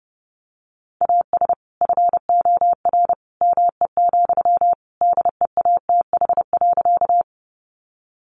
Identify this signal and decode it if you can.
Morse code, heard as a keyed tone, about 30 wpm, 700 Hz. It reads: AS FOR ME, DEATH.
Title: Sentences from Top 1100 Words 30wpm